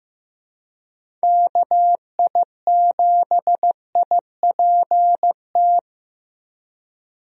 Morse code, heard as a keyed tone, about 15 wpm, 700 Hz. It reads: KI7IPT
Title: USA Call Signs - 6 Characters - 15wpm